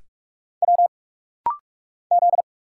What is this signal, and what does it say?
Morse code, heard as a keyed tone, about 46 wpm, 700 Hz. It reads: W Z
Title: Single Letters - Rapid-Fire 45wpm